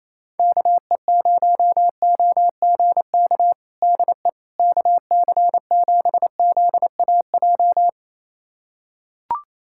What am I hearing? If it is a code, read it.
Morse code, heard as a keyed tone, about 28 wpm, 700 Hz. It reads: KE0OGK DE KC7ZAJ
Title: USA Callsign DE Callsign - 6 Characters - 28wpm